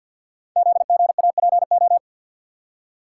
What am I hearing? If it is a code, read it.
Morse code, heard as a keyed tone, about 48 wpm, 700 Hz. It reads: ZGAPO
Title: Code Groups - 5 Letters - 50wpm